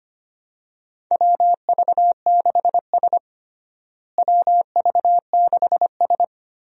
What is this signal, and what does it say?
Morse code, heard as a keyed tone, about 25 wpm, 700 Hz. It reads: WV6S WV6S
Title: USA Call Signs - 4 Characters - Repeated 3x - 25wpm